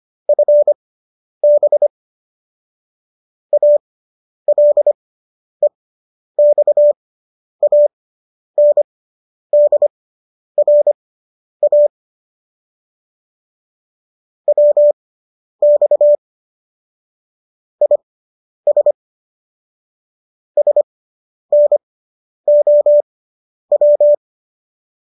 Morse code, heard as a keyed tone, about 25 wpm, spaced as for 10 wpm, 590 Hz. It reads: FB ALEXANDRA WX IS SNOW
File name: qso219 10